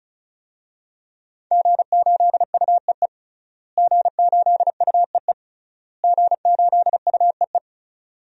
Morse code, heard as a keyed tone, about 34 wpm, 700 Hz. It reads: G8UEE G8UEE G8UEE
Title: DXCC-100 Call Signs - 5 Characters - Repeated 3x - 35wpm